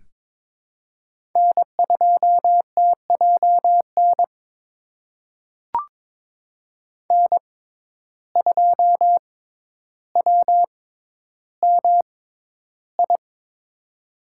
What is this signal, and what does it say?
Morse code, heard as a keyed tone, about 22 wpm, spaced as for 11 wpm, 700 Hz. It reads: N2TJN N2WMI
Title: USA Call Signs - 5 Characters - 6x Character Spacing - 22wpm